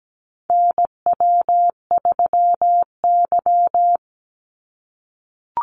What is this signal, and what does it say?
Morse code, heard as a keyed tone, about 17 wpm, 700 Hz. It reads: NW3Y
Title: Call Signs - 17wpm